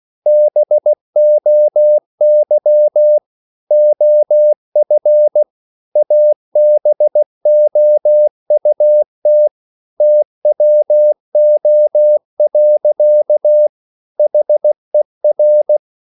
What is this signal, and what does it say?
Morse code, heard as a keyed tone, about 16 wpm, 600 Hz. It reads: BOY OF ABOUT TWO. HER